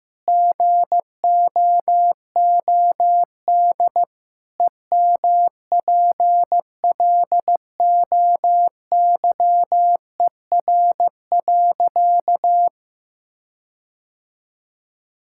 Morse code, heard as a keyed tone, about 15 wpm, 700 Hz. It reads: GOOD EMPLOYER.
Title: Sentences from Top 1600 Words 15wpm